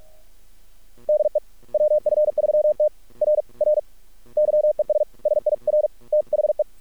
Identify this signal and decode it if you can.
Morse code, heard as a keyed tone, about 50 wpm, 610 Hz. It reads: DE WF3T R R YES SIR THE
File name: archived / CRSnet